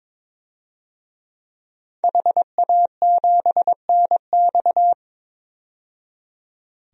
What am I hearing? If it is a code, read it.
Morse code, heard as a keyed tone, about 22 wpm, 700 Hz. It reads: HA7NX
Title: DXCC-100 Call Signs - 5 Characters - 22wpm